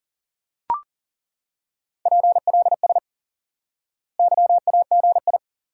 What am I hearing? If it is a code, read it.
Morse code, heard as a keyed tone, about 40 wpm, 700 Hz. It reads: PLS YAGI
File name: IQE Course 3 - Lesson 89 - YAGI - Two Elements - 40wpm